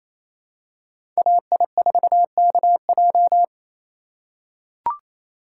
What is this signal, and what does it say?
Morse code, heard as a keyed tone, about 28 wpm, 700 Hz. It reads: AI4KJ
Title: USA Call Signs - 5 Characters - 4x Character Spacing - 28wpm